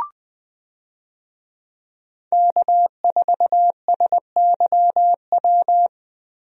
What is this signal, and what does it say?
Morse code, heard as a keyed tone, about 20 wpm, 700 Hz. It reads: K4SYW
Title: USA Call Signs - 5 Characters - Repeated 3x - 20wpm